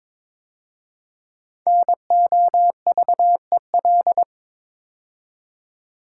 Morse code, heard as a keyed tone, about 22 wpm, 700 Hz. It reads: NOVEL
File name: 5 Letter Words - First 3 Letters Given - 22wpm